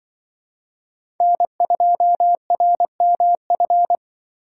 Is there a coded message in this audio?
Yes: Morse code from 1.2 to 4.0 s, heard as a keyed tone, about 25 wpm, 700 Hz. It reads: N2RMF